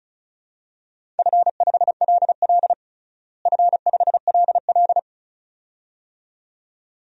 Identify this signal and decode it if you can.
Morse code, heard as a keyed tone, about 36 wpm, 700 Hz. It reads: F5LL F5LL
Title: DXCC-100 Call Signs - 4 Characters - Repeated 2x - 35wpm